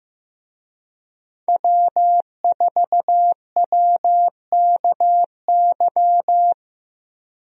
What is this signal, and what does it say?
Morse code, heard as a keyed tone, about 15 wpm, 700 Hz. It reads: W4WKY
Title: USA Call Signs - 5 Characters - Repeated 2x - 15wpm